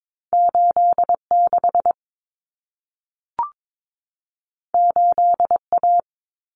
Morse code, heard as a keyed tone, about 22 wpm, 700 Hz. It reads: 86 8A
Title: Common Cut Numbers - 2-Digit 22wpm